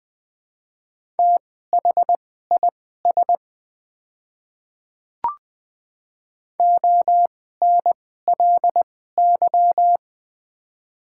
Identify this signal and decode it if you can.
Morse code, heard as a keyed tone, about 21 wpm, spaced as for 15 wpm, 700 Hz. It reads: THIS ONLY
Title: 2x Character Spacing - Top 200 Words - 20wpm